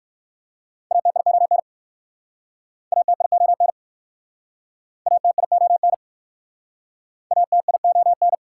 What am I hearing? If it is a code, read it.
Morse code, heard as a keyed tone, about 56 wpm, 700 Hz. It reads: ATION ATION ATION ATION
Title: Speed-Racing Prefixes and Suffixes 45wpm